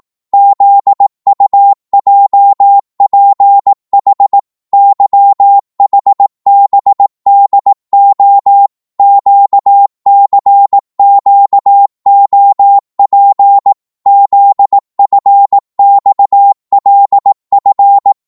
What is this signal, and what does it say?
Morse code, heard as a keyed tone, about 18 wpm, 800 Hz. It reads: ZUJPH YHBDO QCQOP ZFXLF
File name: Morse / ltr-18wpm